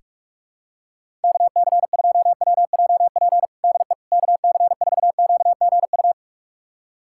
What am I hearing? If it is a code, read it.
Morse code, heard as a keyed tone, about 41 wpm, 700 Hz. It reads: KC2WJP DE KC4QGU
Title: USA Callsign DE Callsign - 6 Characters - 45wpm